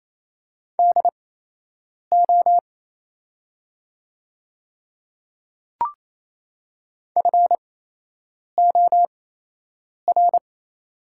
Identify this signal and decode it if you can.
Morse code, heard as a keyed tone, about 28 wpm, spaced as for 8 wpm, 700 Hz. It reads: DO FOR
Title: 8x Character Spacing - Top 100 Words - 28wpm